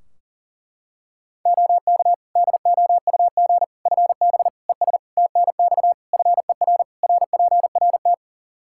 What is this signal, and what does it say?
Morse code, heard as a keyed tone, about 40 wpm, 700 Hz. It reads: OK DOUG FB ES TNX FER RPRT
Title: Practice QSO - Exchange 2 - Info - 40wpm